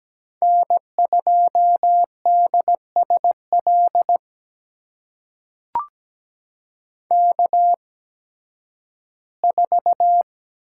Morse code, heard as a keyed tone, about 17 wpm, spaced as for 11 wpm, 700 Hz. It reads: N2DSL K4
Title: USA Call Signs - 5 Characters - 8x Character Spacing - 17wpm